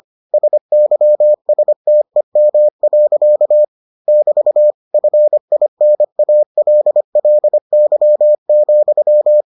Morse code, heard as a keyed tone, about 25 wpm, 600 Hz. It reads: SYSTEM. <BT> FINALLY,